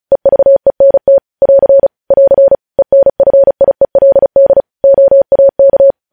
Morse code, heard as a keyed tone, about 36 wpm, 550 Hz. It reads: EVENT <AR> <AR> ENFIELD OAK
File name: podcast / News headlines-1x-35x35